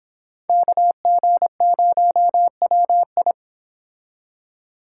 Morse code, heard as a keyed tone, about 25 wpm, 700 Hz. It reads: KG0WI